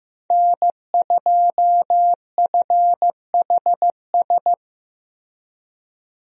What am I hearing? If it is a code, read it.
Morse code, heard as a keyed tone, about 15 wpm, 700 Hz. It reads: N2FHS
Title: USA Call Signs - 5 Characters - 15wpm